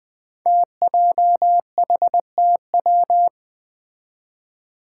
Morse code, heard as a keyed tone, about 20 wpm, 700 Hz. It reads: TJHTW
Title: Five Challenging Letter-Number Combinations 20wpm